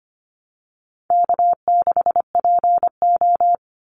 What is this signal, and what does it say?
Morse code, heard as a keyed tone, about 25 wpm, 700 Hz. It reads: K6PO
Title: Easy Call Signs Repeated 2x - 25wpm